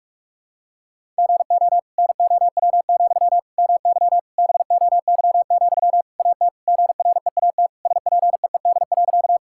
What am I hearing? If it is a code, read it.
Morse code, heard as a keyed tone, about 41 wpm, 700 Hz. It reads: GO NOW, MY BOY, AT GREAT SPEED.
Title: Sentences from Top 1000 Words - Review Part 5 - 45wpm